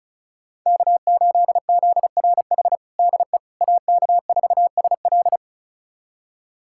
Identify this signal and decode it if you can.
Morse code, heard as a keyed tone, about 36 wpm, 690 Hz. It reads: K8ZRH DE AK4SL
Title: USA Callsign DE Callsign - 5 Characters - 35wpm